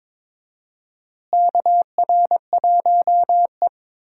Morse code, heard as a keyed tone, about 22 wpm, 700 Hz. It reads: KR1E
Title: USA Call Signs - 4 Characters - Repeated 2x - 22wpm